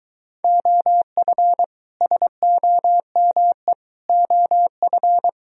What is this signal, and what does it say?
Morse code, heard as a keyed tone, about 23 wpm, 700 Hz. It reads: OF SOME OF